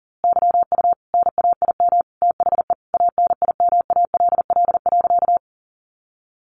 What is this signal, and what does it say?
Morse code, heard as a keyed tone, about 40 wpm, 700 Hz. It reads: YU NAIM THE ANIMALL.
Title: A Sound Of Thunder 40wpm